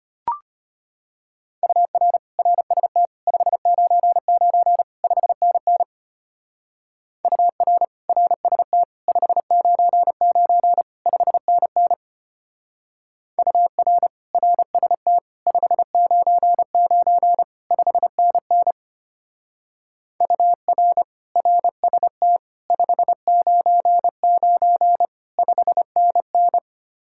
Morse code, heard as a keyed tone, about 32 wpm, 700 Hz. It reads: UR RST 599 5NN UR RST 599 5NN UR RST 599 5NN UR RST 599 5NN